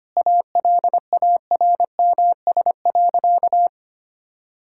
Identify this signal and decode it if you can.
Morse code, heard as a keyed tone, about 25 wpm, 700 Hz. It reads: ALARMS.
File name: Easy Sentences with Hints - 25wpm